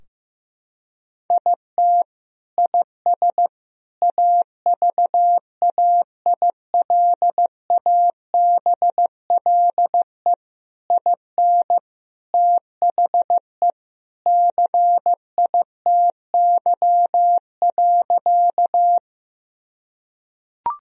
Morse code, heard as a keyed tone, about 15 wpm, 700 Hz. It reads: IT IS AVAILABLE IN THE CITY.
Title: Sentences from Top 1000 Words - Review Part 3 - 15wpm